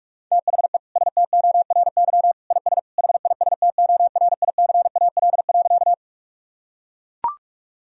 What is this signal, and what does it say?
Morse code, heard as a keyed tone, about 41 wpm, 700 Hz. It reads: THE STORY IS HISTORICAL.
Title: Sentences from Top 1000 Words - Review Part 4 - 45wpm